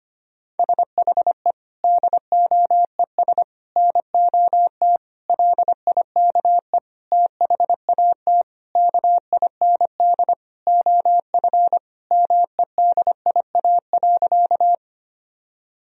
Morse code, heard as a keyed tone, about 25 wpm, 700 Hz. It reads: SHE DOES NOT LIKE THAT KIND OF MEDIA.
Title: Sentences from Top 1000 Words - Review Part 4 - 25wpm